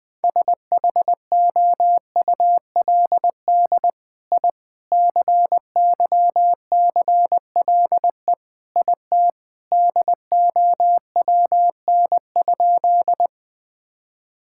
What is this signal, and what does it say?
Morse code, heard as a keyed tone, about 21 wpm, 700 Hz. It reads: SHOULD I CYCLE IT DOWN?
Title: Sentences from Top 1000 Words - Review Part 6 - 20wpm